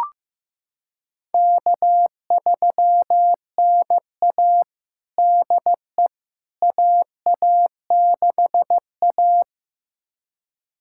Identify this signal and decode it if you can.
Morse code, heard as a keyed tone, about 15 wpm, 700 Hz. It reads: K3NA DE AA6A
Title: USA Callsign DE Callsign - 4 Characters - 15wpm